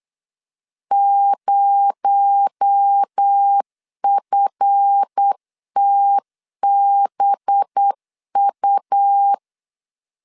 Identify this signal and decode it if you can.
Morse code, heard as a keyed tone, about 8 wpm, 800 Hz. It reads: 0FTBU
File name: LEZIONI / 7 GF9Z / 7 GF9Z 8wpm 3213